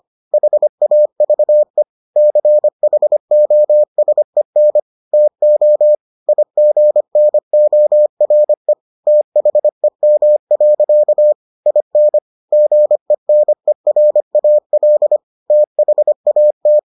Morse code, heard as a keyed tone, about 25 wpm, 600 Hz. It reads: HAVE CHOSEN TO IGNORE THEM. IN GENERAL THAT